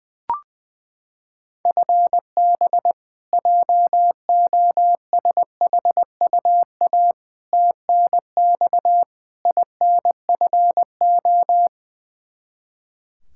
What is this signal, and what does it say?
Morse code, heard as a keyed tone, about 21 wpm, 700 Hz. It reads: FB JOSHUA TNX INFO
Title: K1USN SST QSO Practice 20wpm